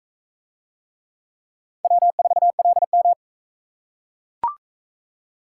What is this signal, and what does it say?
Morse code, heard as a keyed tone, about 40 wpm, 700 Hz. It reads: W4LM